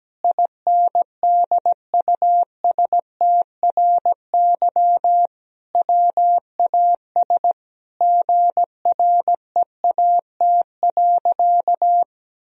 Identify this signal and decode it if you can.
Morse code, heard as a keyed tone, about 17 wpm, 700 Hz. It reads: INDUSTRY WAS GREAT.
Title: Sentences from Top 1000 Words - Review Part 4 - 17wpm